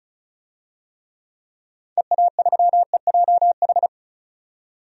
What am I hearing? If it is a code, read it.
Morse code, heard as a keyed tone, about 34 wpm, 700 Hz. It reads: EA3EJH
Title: DXCC-100 Call Signs - 6 Characters - 35wpm